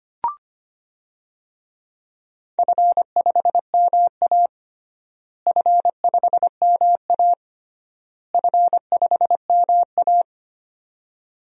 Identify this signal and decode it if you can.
Morse code, heard as a keyed tone, about 25 wpm, 700 Hz. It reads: F5MA F5MA F5MA
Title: DXCC-100 Call Signs - 4 Characters - Repeated 3x - 25wpm